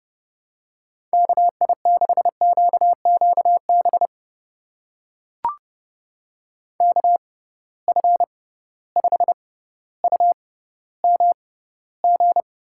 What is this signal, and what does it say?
Morse code, heard as a keyed tone, about 32 wpm, spaced as for 16 wpm, 700 Hz. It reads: KI6QQB KF5UMG
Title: USA Call Signs - 6 Characters - 6x Character Spacing - 30wpm